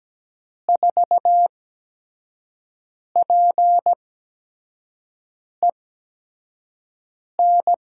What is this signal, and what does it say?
Morse code, heard as a keyed tone, about 17 wpm, spaced as for 5 wpm, 700 Hz. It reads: 4PEN
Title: USA Call Signs - 6 Characters - 8x Character Spacing - 17wpm